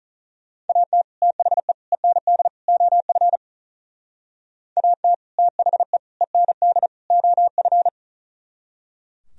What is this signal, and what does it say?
Morse code, heard as a keyed tone, about 44 wpm, 700 Hz. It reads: AT THE END OF AT THE END OF
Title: Speed-Racing - Sets of 4 Words 35wpm